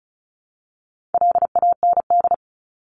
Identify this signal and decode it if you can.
Morse code, heard as a keyed tone, about 34 wpm, 700 Hz. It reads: LAND
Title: Top 500 Words 35wpm